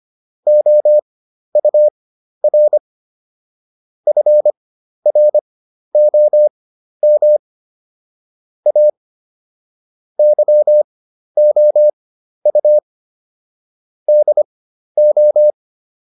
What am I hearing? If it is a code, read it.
Morse code, heard as a keyed tone, about 25 wpm, spaced as for 12 wpm, 600 Hz. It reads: OUR FROM A YOU DO